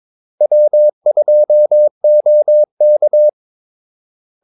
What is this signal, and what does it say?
Morse code, heard as a keyed tone, about 22 wpm, 600 Hz. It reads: W2OK
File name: Callsigns-22wpm